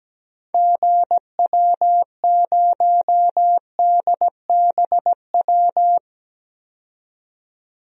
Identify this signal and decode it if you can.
Morse code, heard as a keyed tone, about 17 wpm, 700 Hz. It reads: GW0DBW
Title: DXCC-100 Call Signs - 6 Characters - 17wpm